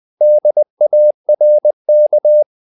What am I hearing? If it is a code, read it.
Morse code, heard as a keyed tone, about 20 wpm, 600 Hz. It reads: DARK
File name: the old man and the sea part 2 20wpm